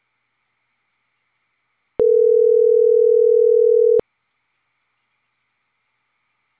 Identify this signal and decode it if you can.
Ringback tone (North America), 440 + 480 Hz.